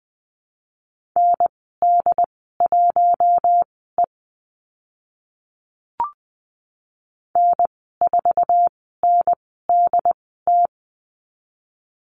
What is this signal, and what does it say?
Morse code, heard as a keyed tone, about 20 wpm, spaced as for 15 wpm, 700 Hz. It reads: ND1E N4NDT
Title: 069 Lesson 12g - Letter d - Call Signs 20wpm